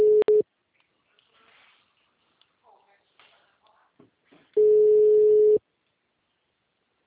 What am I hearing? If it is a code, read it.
Ringback tone (Europe, 425 Hz), with speech.
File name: out- -2002- - - .55197